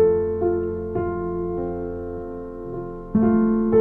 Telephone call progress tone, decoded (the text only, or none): dial tone